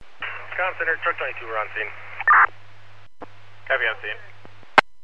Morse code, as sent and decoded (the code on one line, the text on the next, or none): none